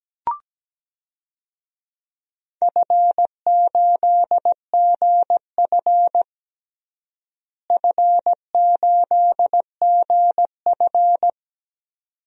Morse code, heard as a keyed tone, about 17 wpm, 700 Hz. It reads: F8GF F8GF
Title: DXCC-100 Call Signs - 4 Characters - Repeated 2x - 17wpm